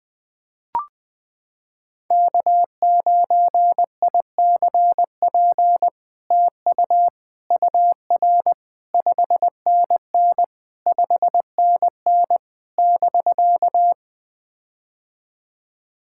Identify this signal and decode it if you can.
Morse code, heard as a keyed tone, about 20 wpm, 700 Hz. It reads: K9ICP TU UR 5NN 5NN <BK>
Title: POTA QSO Practice - 20wpm